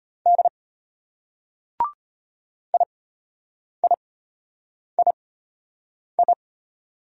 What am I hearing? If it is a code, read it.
Morse code, heard as a keyed tone, about 33 wpm, spaced as for 8 wpm, 700 Hz. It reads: D IIII